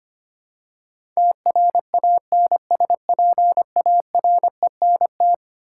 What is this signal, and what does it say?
Morse code, heard as a keyed tone, about 25 wpm, 700 Hz. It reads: TRANSPARENT
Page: morse-code-ninja-practice-25wpm